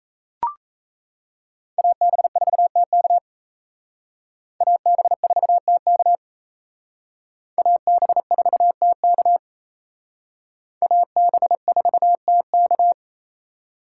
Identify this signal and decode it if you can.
Morse code, heard as a keyed tone, about 35 wpm, 700 Hz. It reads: AB4TK AB4TK AB4TK AB4TK
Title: Speed Racing - USA 2x2 Call Signs 28wpm